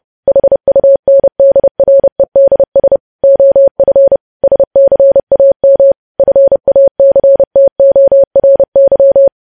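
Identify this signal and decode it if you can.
Morse code, heard as a keyed tone, about 30 wpm, 550 Hz. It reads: HUNDREDS OF SCAM FACTORY
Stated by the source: Morse code news headlines sent at 30 words per minute.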